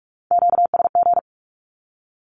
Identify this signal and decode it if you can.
Morse code, heard as a keyed tone, about 44 wpm, 700 Hz. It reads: QSG
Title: Q-Codes Repeated 3x 45wpm